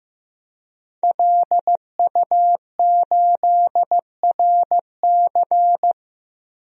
Morse code, heard as a keyed tone, about 15 wpm, 700 Hz. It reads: LU8RC
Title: DXCC-100 Call Signs - 5 Characters - 15wpm